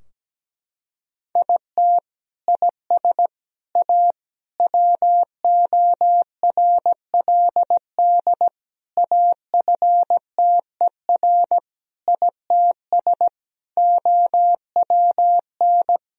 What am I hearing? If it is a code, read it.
Morse code, heard as a keyed tone, about 17 wpm, 700 Hz. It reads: IT IS A WORLD AFTER ITS OWN
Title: Sentences from Top 1000 Words - Review Part 4 - 17wpm